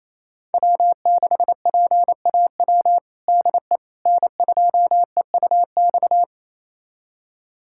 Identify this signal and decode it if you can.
Morse code, heard as a keyed tone, about 28 wpm, 700 Hz. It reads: W6PAW DE N2EUX
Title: USA Callsign DE Callsign - 5 Characters - 28wpm